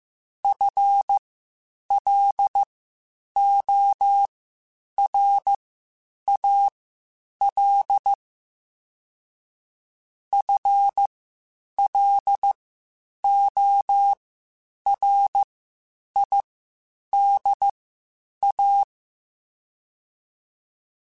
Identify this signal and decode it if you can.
Morse code, heard as a keyed tone, about 15 wpm, spaced as for 8 wpm, 800 Hz. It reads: FLORAL FLORIDA
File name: Spaced / FL Runs sp mp3s / FL Runs sp @13